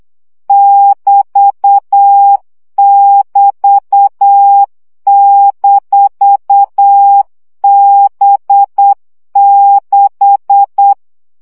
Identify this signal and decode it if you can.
Morse code, heard as a keyed tone, about 8 wpm, 800 Hz. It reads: <BT><BT>-B6
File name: CW gyakorlas b 6 egyenlo minusz 7 baud